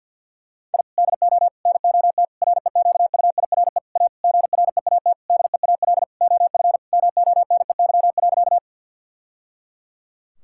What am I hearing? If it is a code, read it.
Morse code, heard as a keyed tone, about 49 wpm, 700 Hz. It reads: I DO NOT REQUIRE A GREAT DEAL OF MONEY.
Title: Sentences from Top 1000 Words - Review Part 6 - 50wpm